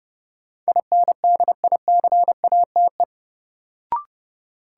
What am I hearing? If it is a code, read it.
Morse code, heard as a keyed tone, about 30 wpm, 700 Hz. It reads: INDICATE
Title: Sentences from Top 1300 Words - New Word Drill Alt - 30wpm